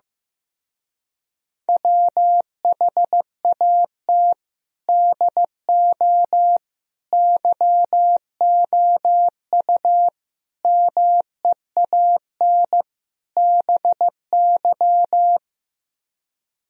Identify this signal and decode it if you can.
Morse code, heard as a keyed tone, about 16 wpm, 700 Hz. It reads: WHAT DO YOU MEAN BY
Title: Speed-Racing - Sets of 5 Words 15wpm